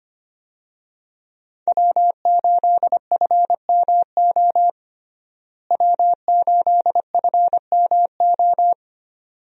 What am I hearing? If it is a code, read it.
Morse code, heard as a keyed tone, about 25 wpm, 700 Hz. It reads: W8FMO W8FMO
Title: USA Call Signs - 5 Characters - Repeated 2x - 25wpm